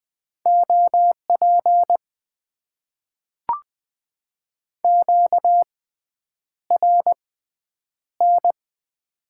Morse code, heard as a keyed tone, about 20 wpm, spaced as for 9 wpm, 700 Hz. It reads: OP QRN
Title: Top 90 QSO Elements - 6x Character Spacing - 20wpm